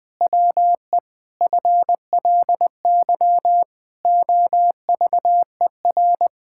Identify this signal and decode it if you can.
Morse code, heard as a keyed tone, about 20 wpm, 700 Hz. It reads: WE FLY OVER
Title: Pangram Challenge v2 - 20wpm